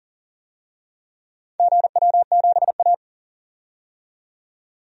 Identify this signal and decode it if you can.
Morse code, heard as a keyed tone, about 40 wpm, 700 Hz. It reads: GW7A
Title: DXCC-100 Call Signs - 4 Characters - 40wpm